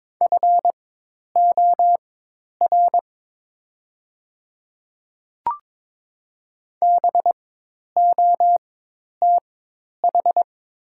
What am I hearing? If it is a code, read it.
Morse code, heard as a keyed tone, about 22 wpm, spaced as for 10 wpm, 700 Hz. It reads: FOR BOTH
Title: 4x Character Spacing - Top 300 Words - 22wpm